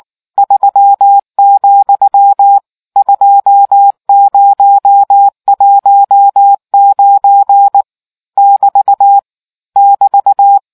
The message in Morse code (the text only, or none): 3, 2019 <BT> <BT>